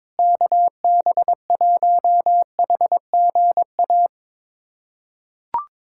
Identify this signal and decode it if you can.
Morse code, heard as a keyed tone, about 22 wpm, 700 Hz. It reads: KB1HGA
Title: USA Callsign DE Callsign - 6 Characters - 22wpm